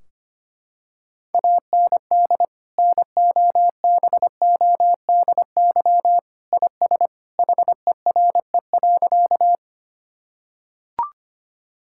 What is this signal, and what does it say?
Morse code, heard as a keyed tone, about 25 wpm, 700 Hz. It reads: AND NOBODY IS HERE.
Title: Sentences from Top 1600 Words 25wpm